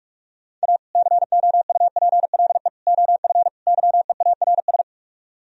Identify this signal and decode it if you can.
Morse code, heard as a keyed tone, about 44 wpm, 700 Hz. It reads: A COUPLE OF YEARS